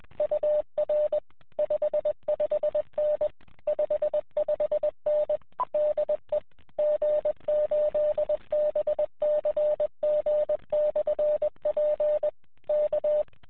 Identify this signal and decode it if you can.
Morse code, heard as a keyed tone, about 20 wpm, 600 Hz. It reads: UR 55N 55N DE G8BCG/P K
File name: fs-w6jkv-qso